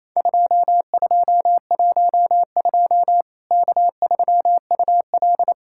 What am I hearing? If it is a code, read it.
Morse code, heard as a keyed tone, about 28 wpm, 700 Hz. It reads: 2212 K3UL
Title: Worked All Europe DX Contest - QTC Exchange - 28wpm